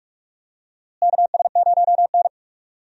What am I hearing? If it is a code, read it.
Morse code, heard as a keyed tone, about 42 wpm, 700 Hz. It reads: KS0N
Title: USA Call Signs - 4 Characters - Repeated 2x - 45wpm